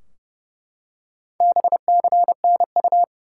Morse code, heard as a keyed tone, about 32 wpm, 700 Hz. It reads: BCNU
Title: Top 90 QSO Elements 30wpm